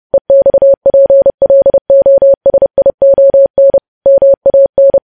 Morse code, heard as a keyed tone, about 30 wpm, 550 Hz. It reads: EXPLOSION MAN
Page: news headlines in morse code at 30 wpm